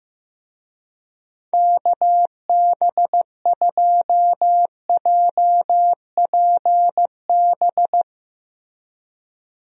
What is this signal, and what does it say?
Morse code, heard as a keyed tone, about 14 wpm, 700 Hz. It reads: KB2JPB
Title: USA Call Signs - 6 Characters - Repeated 3x - 15wpm